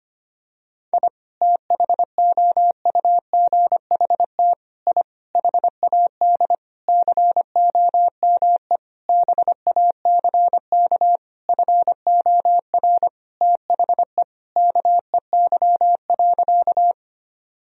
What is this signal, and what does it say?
Morse code, heard as a keyed tone, about 25 wpm, 700 Hz. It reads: I THOUGHT I HAD COME BACK FOR THE KEY.
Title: Sentences from Top 1000 Words - Review Part 4 - 25wpm